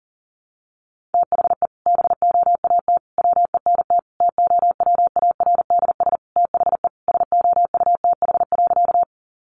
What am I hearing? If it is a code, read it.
Morse code, heard as a keyed tone, about 37 wpm, 700 Hz. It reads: THE BOAT WENT TOWARDS THE SOUTH.
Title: Sentences from Top 1100 Words 40wpm